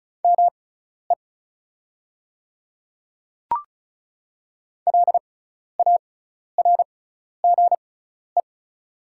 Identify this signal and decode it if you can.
Morse code, heard as a keyed tone, about 36 wpm, spaced as for 12 wpm, 690 Hz. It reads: ME LARGE
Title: 6x Character Spacing - Top 300 Words - 35wpm